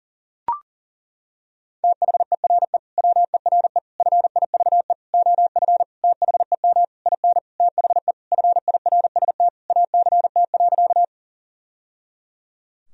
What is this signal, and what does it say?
Morse code, heard as a keyed tone, about 40 wpm, 700 Hz. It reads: THERE WERE FIVE OF THEM IN THE FIRST ACT.
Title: Sentences from Top 1000 Words - Review Part 2 - 40wpm